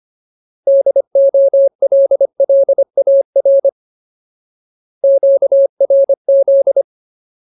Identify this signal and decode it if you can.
Morse code, heard as a keyed tone, about 25 wpm, 550 Hz. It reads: DOLLAR QRZ
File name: Ultimate Word List / 25WPM Ultimate